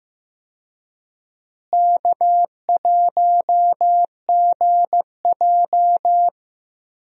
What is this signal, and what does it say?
Morse code, heard as a keyed tone, about 15 wpm, 700 Hz. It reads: K1GJ
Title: USA Call Signs - 4 Characters - Repeated 2x - 15wpm